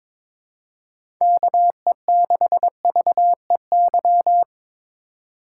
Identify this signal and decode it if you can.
Morse code, heard as a keyed tone, about 22 wpm, 700 Hz. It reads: KE6VEY
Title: USA Call Signs - 6 Characters - 2x Character Spacing - 22wpm